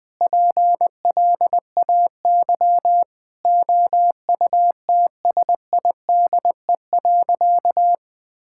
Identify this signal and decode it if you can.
Morse code, heard as a keyed tone, about 20 wpm, 700 Hz. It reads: PLAY OUTSIDE.
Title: Sentences with 5-Word Phrases - 20wpm